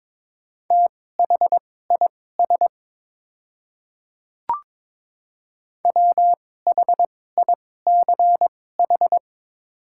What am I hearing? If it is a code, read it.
Morse code, heard as a keyed tone, about 22 wpm, spaced as for 16 wpm, 700 Hz. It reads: THIS WHICH
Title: 2x Character Spacing - Top 300 Words - 22wpm